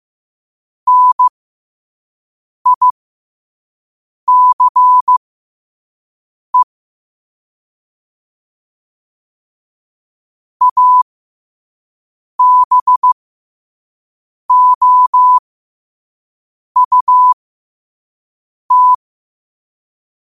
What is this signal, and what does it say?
Morse code, heard as a keyed tone, about 15 wpm, spaced as for 5 wpm, 1000 Hz. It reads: NICE ABOUT
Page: Quotes for Fri, 15 Aug 2025 in Morse Code at 5 words per minute.